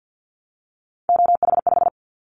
Morse code, heard as a keyed tone, about 50 wpm, 700 Hz. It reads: OH5
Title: Callsign Prefixes for Top 25 DXCC - 50wpm